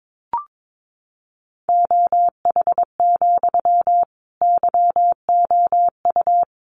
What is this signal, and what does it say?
Morse code, heard as a keyed tone, about 22 wpm, 700 Hz. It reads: OH, YOU
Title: Sentences from Top 500 Words 22wpm